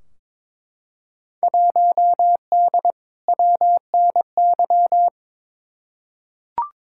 Morse code, heard as a keyed tone, about 21 wpm, 700 Hz. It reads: 1D WNY
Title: Field Day Exchanges - 22wpm